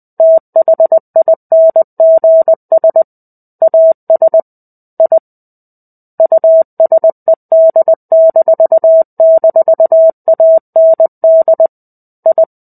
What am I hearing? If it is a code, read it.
Morse code, heard as a keyed tone, about 20 wpm, 650 Hz. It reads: THINGS AS I USED--AND I
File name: 05-Chapter-05 20 WPM